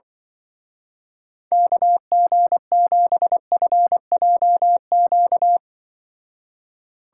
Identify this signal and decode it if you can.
Morse code, heard as a keyed tone, about 24 wpm, 700 Hz. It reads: KG7FJQ
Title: Speed Racing - USA Call Signs 20wpm